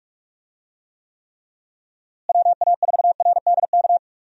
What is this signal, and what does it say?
Morse code, heard as a keyed tone, about 42 wpm, 700 Hz. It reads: WA4RDK
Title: USA Call Signs - 6 Characters - Repeated 3x - 45wpm